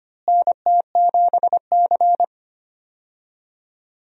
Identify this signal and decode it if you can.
Morse code, heard as a keyed tone, about 25 wpm, 700 Hz. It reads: NT7C
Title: USA Call Signs - 4 Characters - Repeated 3x - 25wpm